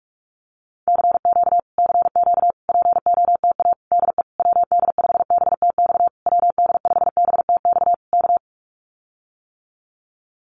Morse code, heard as a keyed tone, about 44 wpm, 700 Hz. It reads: CQ CQ POTA DE WD5BTX WD5BTX K
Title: Common QSO Phrases Encore 45wpm